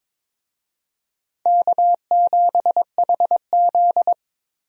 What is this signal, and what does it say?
Morse code, heard as a keyed tone, about 22 wpm, 700 Hz. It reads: K7HZ
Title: USA Call Signs - 4 Characters - Repeated 3x - 22wpm